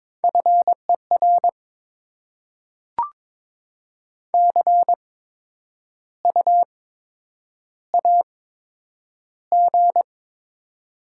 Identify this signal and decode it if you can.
Morse code, heard as a keyed tone, about 22 wpm, spaced as for 9 wpm, 700 Hz. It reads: FER CU AG
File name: Top 20 QSO Elements - 8x Character Spacing - 22wpm